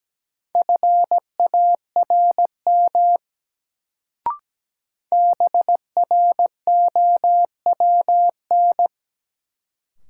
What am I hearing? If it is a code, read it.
Morse code, heard as a keyed tone, about 17 wpm, 700 Hz. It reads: FARM BROWN
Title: Sentences from Top 1200 Words - New Word Drill Alt - 17wpm